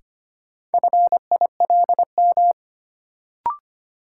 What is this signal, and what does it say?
Morse code, heard as a keyed tone, about 25 wpm, 700 Hz. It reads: FILM
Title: Sentences from Top 1000 Words - Review Part 3 - New Word Drill - 25wpm